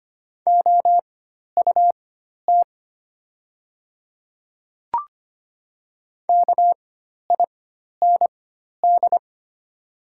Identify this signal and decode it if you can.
Morse code, heard as a keyed tone, about 25 wpm, spaced as for 12 wpm, 700 Hz. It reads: OUT KIND
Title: 4x Character Spacing - Top 300 Words - 25wpm